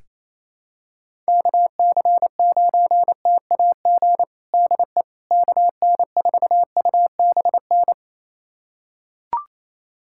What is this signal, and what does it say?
Morse code, heard as a keyed tone, about 27 wpm, 700 Hz. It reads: KC9TAG DE KN4UBN
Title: USA Callsign DE Callsign - 6 Characters - 28wpm